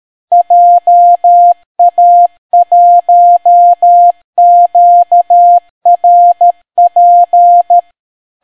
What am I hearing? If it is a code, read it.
Morse code, heard as a keyed tone, about 13 wpm, 700 Hz. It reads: JA1QRP